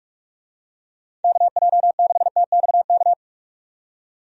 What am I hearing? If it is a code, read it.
Morse code, heard as a keyed tone, about 44 wpm, 700 Hz. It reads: KJ6TXK
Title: USA Call Signs - 6 Characters - 45wpm